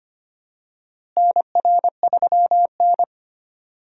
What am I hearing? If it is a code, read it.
Morse code, heard as a keyed tone, about 25 wpm, 700 Hz. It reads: NR3N
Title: USA Call Signs - 4 Characters - Repeated 2x - 25wpm